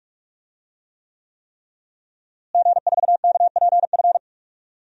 Morse code, heard as a keyed tone, about 46 wpm, 700 Hz. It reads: G4KPF
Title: DXCC-100 Call Signs - 5 Characters - Repeated 3x - 45wpm